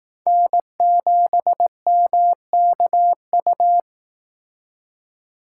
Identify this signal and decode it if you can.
Morse code, heard as a keyed tone, about 18 wpm, 700 Hz. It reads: N7MKU
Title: Speed Racing - USA 1x3 Call Signs 15wpm